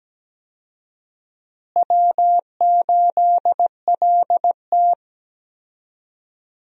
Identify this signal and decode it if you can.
Morse code, heard as a keyed tone, about 17 wpm, 700 Hz. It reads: W8LT
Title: USA Call Signs - 4 Characters - Repeated 3x - 17wpm